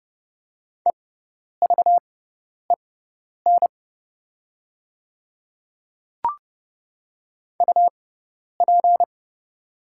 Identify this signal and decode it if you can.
Morse code, heard as a keyed tone, about 30 wpm, spaced as for 10 wpm, 700 Hz. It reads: EVEN UP
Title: 6x Character Spacing - Top 300 Words - 30wpm